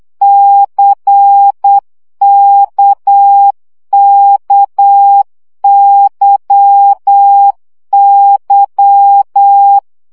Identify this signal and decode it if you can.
Morse code, heard as a keyed tone, about 8 wpm, 800 Hz. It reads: CKKYY
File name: CW gyakorlas k c y 7 baud